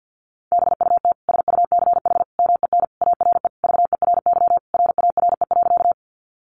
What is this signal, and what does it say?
Morse code, heard as a keyed tone, about 50 wpm, 700 Hz. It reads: BUT SUCH MEN ARE VERY RARE.
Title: Sentences from Top 1100 Words 50wpm